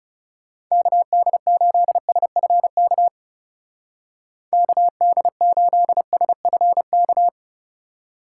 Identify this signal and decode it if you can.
Morse code, heard as a keyed tone, about 37 wpm, 700 Hz. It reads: KD8SFK KD8SFK
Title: Speed Racing - USA 2x3 Call Signs 30wpm